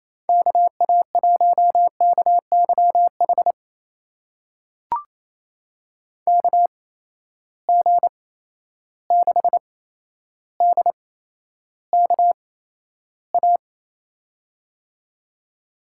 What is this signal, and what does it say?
Morse code, heard as a keyed tone, about 28 wpm, spaced as for 12 wpm, 700 Hz. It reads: KA1KYH KG6DKA
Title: USA Call Signs - 6 Characters - 8x Character Spacing - 28wpm